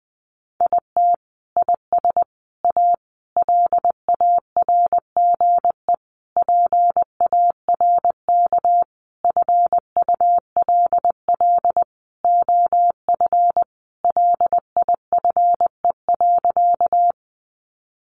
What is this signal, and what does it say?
Morse code, heard as a keyed tone, about 20 wpm, 700 Hz. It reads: IT IS A LARGE PARK FULL OF LIFE.
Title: Sentences from Top 1300 Words 20wpm